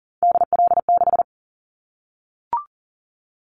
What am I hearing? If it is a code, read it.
Morse code, heard as a keyed tone, about 40 wpm, 700 Hz. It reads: DL6
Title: Callsign Prefixes for Top 25 DXCC - Repeated 3x - 40wpm